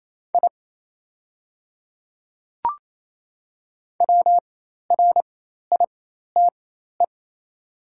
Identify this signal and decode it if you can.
Morse code, heard as a keyed tone, about 28 wpm, spaced as for 13 wpm, 690 Hz. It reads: I WRITE
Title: 4x Character Spacing - Top 100 Words - 28wpm